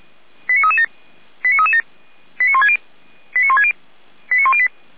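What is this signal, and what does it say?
Five-tone selective call (ZVEI), heard as quick runs of five tones, tones of about 70 ms. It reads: 89337 89337 89160 89160 89108
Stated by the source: ZVEI five-tone call received with an SDR.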